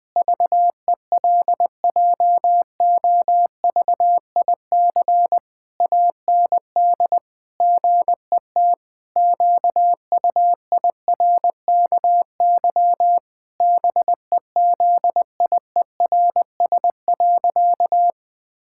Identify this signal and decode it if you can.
Morse code, heard as a keyed tone, about 20 wpm, 700 Hz. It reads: VELJOVIC AND GET QUIRKY BEZIERS.
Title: Pangram Challenge v2 - 20wpm